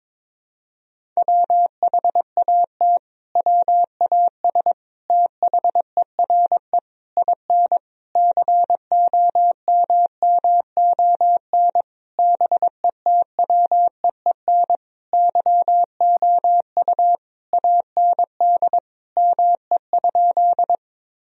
Morse code, heard as a keyed tone, about 22 wpm, 700 Hz. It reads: WHAT WAS THERE IN COMMON BETWEEN YOU AND ME?
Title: Sentences from Top 1000 Words - Review Part 3 - 22wpm